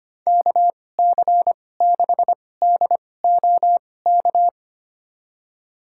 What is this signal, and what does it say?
Morse code, heard as a keyed tone, about 25 wpm, spaced as for 18 wpm, 700 Hz. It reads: KC6DOK
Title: USA Call Signs - 6 Characters - 2x Character Spacing - 25wpm